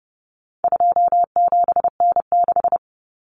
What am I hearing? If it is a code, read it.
Morse code, heard as a keyed tone, about 30 wpm, 700 Hz. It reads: 27N6
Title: Common Cut Numbers - 4-Digit 30wpm